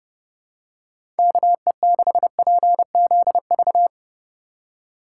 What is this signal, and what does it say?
Morse code, heard as a keyed tone, about 32 wpm, 700 Hz. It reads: KE6PZV
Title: USA Call Signs - 3x Character Spacing - 30wpm